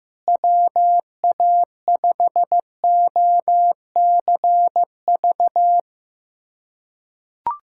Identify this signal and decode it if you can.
Morse code, heard as a keyed tone, about 15 wpm, 700 Hz. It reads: WA5OCV
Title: USA Callsign DE Callsign - 6 Characters - 15wpm